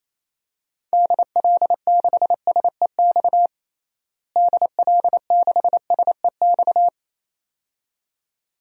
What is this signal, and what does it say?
Morse code, heard as a keyed tone, about 28 wpm, 700 Hz. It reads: DL6SEX DL6SEX
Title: DXCC-100 Call Signs - 6 Characters - Repeated 2x - 28wpm